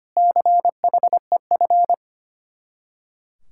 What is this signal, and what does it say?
Morse code, heard as a keyed tone, about 32 wpm, 700 Hz. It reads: CHEF